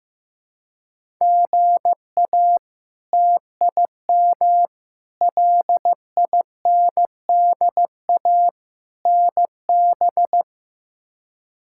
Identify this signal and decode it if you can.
Morse code, heard as a keyed tone, about 15 wpm, 700 Hz. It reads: GA TIM LINDA NB
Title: K1USN SST QSO Practice - 15wpm